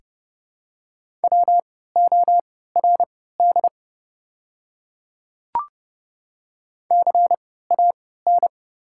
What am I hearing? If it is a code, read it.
Morse code, heard as a keyed tone, about 32 wpm, spaced as for 17 wpm, 700 Hz. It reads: WORD CAN
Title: 3x Character Spacing - Top 200 Words - 30wpm